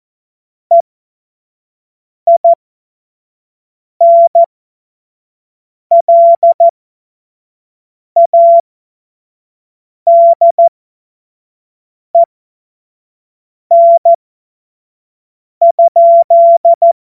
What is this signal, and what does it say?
Morse code, heard as a keyed tone, about 14 wpm, spaced as for 5 wpm, 680 Hz. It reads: EINLADEN?
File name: Morsekurs / Witze / Witz 11. Januar 2022, 25 F70